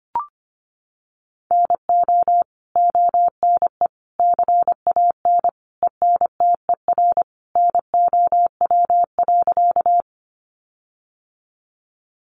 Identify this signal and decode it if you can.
Morse code, heard as a keyed tone, about 25 wpm, 700 Hz. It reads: NO ONE CAN ENTER NOW.
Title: Sentences from Top 1300 Words 25wpm